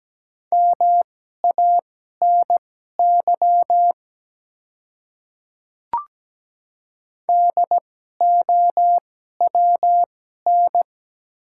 Morse code, heard as a keyed tone, about 17 wpm, spaced as for 12 wpm, 700 Hz. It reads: MANY DOWN
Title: 2x Character Spacing - Top 100 Words - 17wpm